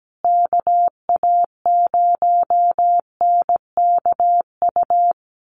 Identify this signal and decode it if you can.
Morse code, heard as a keyed tone, about 17 wpm, 700 Hz. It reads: KA0NKU
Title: USA Call Signs Repeated 3x 17wpm